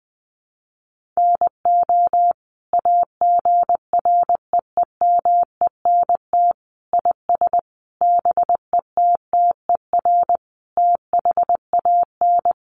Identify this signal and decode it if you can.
Morse code, heard as a keyed tone, about 20 wpm, 700 Hz. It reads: NO AGREEMENT IS BETTER THAN
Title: Sentences from Top 1500 Words 20wpm